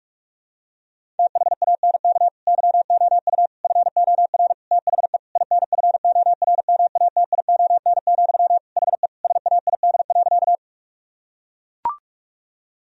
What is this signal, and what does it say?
Morse code, heard as a keyed tone, about 41 wpm, 700 Hz. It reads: THANK YOU FOR THE INFORMATION, HE SAID.
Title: Sentences from Top 1000 Words - Review Part 3 - 45wpm